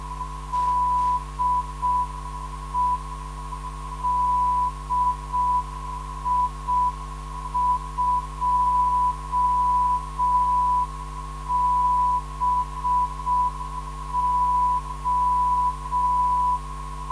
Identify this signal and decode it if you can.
Morse code, heard as a keyed tone, about 6 wpm, 990 Hz. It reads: DE DI2BO
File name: ndb zvok / DI2BO 440kHz